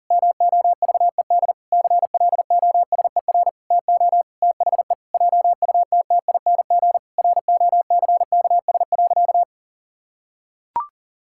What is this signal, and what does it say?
Morse code, heard as a keyed tone, about 40 wpm, 700 Hz. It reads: MOVED CLOSER TO THE JUTTING ROCKS.
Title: Pangram Challenge v2 - 40wpm